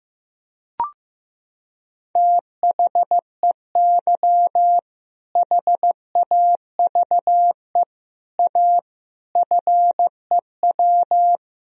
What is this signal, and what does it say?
Morse code, heard as a keyed tone, about 15 wpm, 700 Hz. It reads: THEY HAVE A FEW
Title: Sentences from Top 1000 Words - Review Part 1 - 15wpm